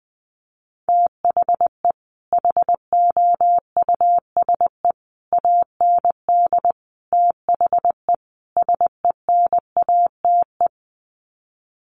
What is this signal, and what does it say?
Morse code, heard as a keyed tone, about 20 wpm, 700 Hz. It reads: THE HOUSE AND THE SENATE
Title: Sets of 5 Words Encore2 20wpm